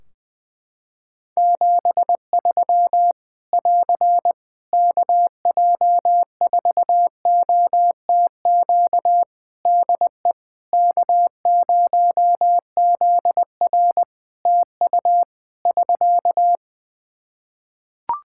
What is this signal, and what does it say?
Morse code, heard as a keyed tone, about 20 wpm, 700 Hz. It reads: 73 <AR> KJ4OTQ DE K0ZR TU <SK>
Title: Practice QSO - Exchange 4 - Ending - 20wpm